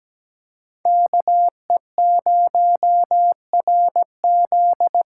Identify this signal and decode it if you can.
Morse code, heard as a keyed tone, about 20 wpm, 700 Hz. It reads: KE0RZ
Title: Speed Racing - USA Call Signs 17wpm